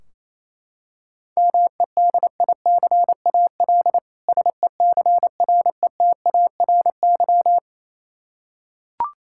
Morse code, heard as a keyed tone, about 28 wpm, 700 Hz. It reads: MEDICAL SECRETARY
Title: Practice QSO - Exchange 3 - More Info - 28wpm